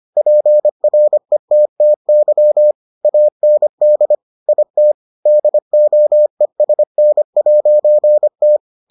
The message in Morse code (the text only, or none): PRETTY AND IT DOESN'T